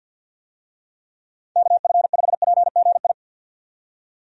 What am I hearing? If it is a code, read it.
Morse code, heard as a keyed tone, about 51 wpm, 700 Hz. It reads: KF5PGI
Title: USA Call Signs - 6 Characters - 50wpm